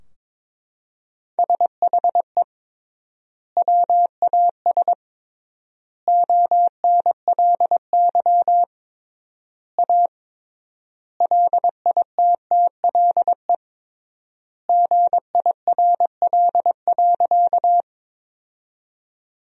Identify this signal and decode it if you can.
Morse code, heard as a keyed tone, about 22 wpm, 700 Hz. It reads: SHE WAS ONLY A LITTLE GIRL.
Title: 3x Word-Spacing - Sentences from Top 400 Words - 22wpm